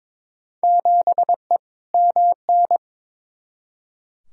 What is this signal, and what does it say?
Morse code, heard as a keyed tone, about 21 wpm, 700 Hz. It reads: 7E MN
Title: Field Day Exchanges Encore - 22wpm